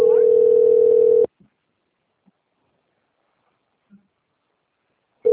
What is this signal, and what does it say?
Ringback tone (North America), 440 + 480 Hz.